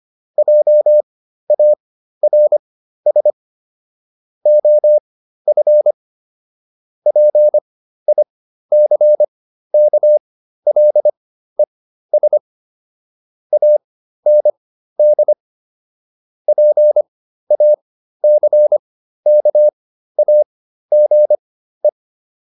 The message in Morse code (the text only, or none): JARS OF PICKLES AND PACKAGE